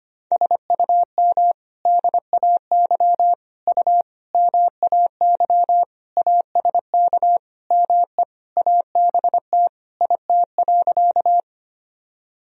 Morse code, heard as a keyed tone, about 25 wpm, 700 Hz. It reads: SUM DAY U MAY ASK ME ABT IT.
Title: Sentences from Top 200 Words and Abbreviations 25wpm